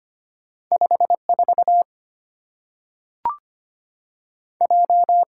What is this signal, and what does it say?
Morse code, heard as a keyed tone, about 25 wpm, 700 Hz. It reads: 54 J
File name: Two Letter-Number Combinations - 25wpm